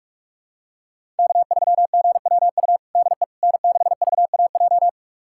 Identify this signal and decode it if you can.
Morse code, heard as a keyed tone, about 46 wpm, 700 Hz. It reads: K3GWU DE N6VAJ
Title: USA Callsign DE Callsign - 5 Characters - 45wpm